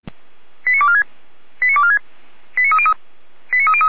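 Five-tone selective call (ZVEI), heard as quick runs of five tones, tones of about 70 ms. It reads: 89246 89246 89303 89303
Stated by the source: ZVEI five-tone selective call 89168 received by SDR.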